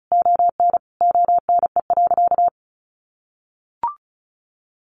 Morse code, heard as a keyed tone, about 34 wpm, 700 Hz. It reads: ON ONE.
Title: Sentences from Top 1500 Words 35wpm